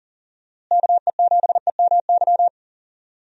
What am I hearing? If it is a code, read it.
Morse code, heard as a keyed tone, about 40 wpm, 700 Hz. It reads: KE7EMY
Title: USA Call Signs - 6 Characters - Repeated 3x - 40wpm